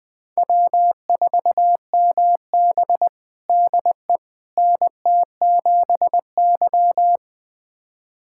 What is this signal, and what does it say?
Morse code, heard as a keyed tone, about 21 wpm, 700 Hz. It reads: W4MB DE NT7Y
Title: USA Callsign DE Callsign - 4 Characters - 20wpm